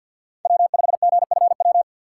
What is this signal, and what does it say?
Morse code, heard as a keyed tone, about 53 wpm, 700 Hz. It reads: W5GFW
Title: USA Call Signs - 5 Characters - 6x Character Spacing - 50wpm